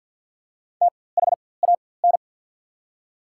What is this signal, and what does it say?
Morse code, heard as a keyed tone, about 53 wpm, spaced as for 24 wpm, 690 Hz. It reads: THAN
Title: 4x Character Spacing - Top 100 Words - 50wpm